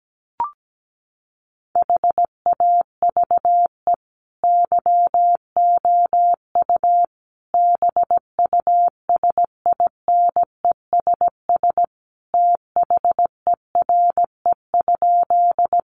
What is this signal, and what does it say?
Morse code, heard as a keyed tone, about 17 wpm, 700 Hz. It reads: HAVE YOU BUSINESS THERE?
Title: Sentences from Top 600 Words 17wpm